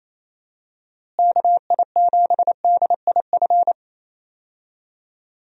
Morse code, heard as a keyed tone, about 28 wpm, 700 Hz. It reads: KI7DIF
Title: USA Call Signs - 6 Characters - Repeated 3x - 28wpm